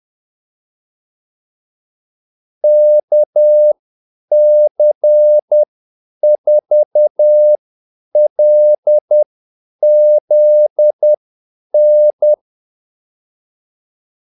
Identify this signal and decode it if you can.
Morse code, heard as a keyed tone, about 10 wpm, spaced as for 8 wpm, 600 Hz. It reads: KC4LZN
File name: Callsigns-10-8wpm